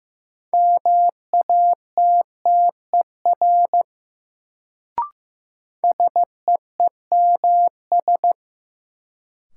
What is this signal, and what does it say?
Morse code, heard as a keyed tone, about 15 wpm, 700 Hz. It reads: MATTER SEEMS
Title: Sets of 3 Words - New Word Drill - 15wpm